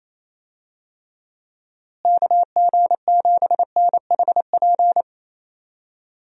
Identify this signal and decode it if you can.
Morse code, heard as a keyed tone, about 28 wpm, 700 Hz. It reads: KG7NHP
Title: USA Call Signs - 6 Characters - Repeated 3x - 28wpm